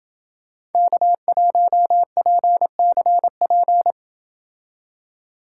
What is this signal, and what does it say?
Morse code, heard as a keyed tone, about 28 wpm, 700 Hz. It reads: K1PCP